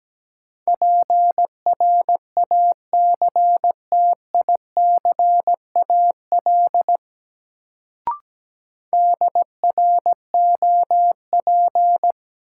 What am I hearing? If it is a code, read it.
Morse code, heard as a keyed tone, about 17 wpm, 700 Hz. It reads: PRACTICAL DROP
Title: Sentences from Top 1400 Words - New Word Drill - 17wpm